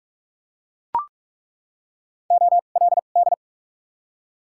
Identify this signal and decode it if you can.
Morse code, heard as a keyed tone, about 42 wpm, spaced as for 31 wpm, 700 Hz. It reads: OLD
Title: 2x Character Spacing - Top 200 Words - 45wpm